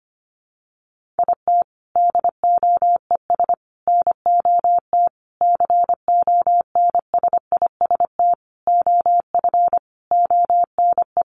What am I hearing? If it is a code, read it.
Morse code, heard as a keyed tone, about 25 wpm, 700 Hz. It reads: IT DOES NOT CONSIST OF ONE
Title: Sentences from Top 1400 Words 25wpm